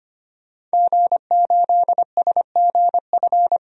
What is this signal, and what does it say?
Morse code, heard as a keyed tone, about 25 wpm, 700 Hz. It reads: G8SGF
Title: DXCC-100 Call Signs - 5 Characters - 25wpm